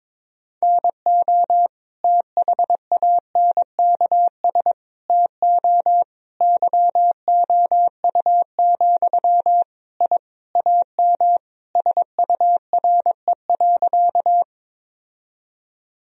Morse code, heard as a keyed tone, about 22 wpm, 700 Hz. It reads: NO THANKS TO YOU, I AM SURE.
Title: Sentences from Top 1000 Words - Review Part 3 - 22wpm